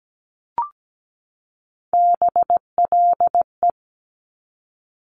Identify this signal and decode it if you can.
Morse code, heard as a keyed tone, about 17 wpm, 700 Hz. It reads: BLE
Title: Top Three Letter Combinations in Words 17wpm